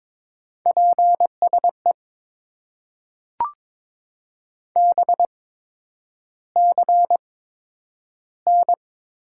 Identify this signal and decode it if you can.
Morse code, heard as a keyed tone, about 22 wpm, spaced as for 9 wpm, 700 Hz. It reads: PSE BCN
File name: Top 90 QSO Elements - 8x Character Spacing - 22wpm